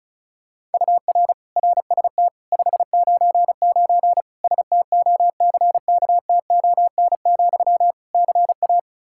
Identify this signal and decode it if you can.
Morse code, heard as a keyed tone, about 36 wpm, 700 Hz. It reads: UR RST 599 STOCKTON, CA
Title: K1USN SST QSO Practice 35wpm